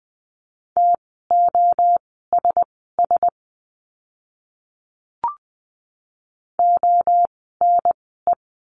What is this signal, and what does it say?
Morse code, heard as a keyed tone, about 20 wpm, spaced as for 15 wpm, 700 Hz. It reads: TOSS ONE
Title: 061 Lesson 11f - Letter h - Words 20wpm